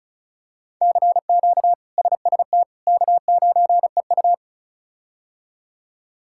Morse code, heard as a keyed tone, about 36 wpm, 700 Hz. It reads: CQ SST K9EU
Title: K1USN SST QSO Practice - 35wpm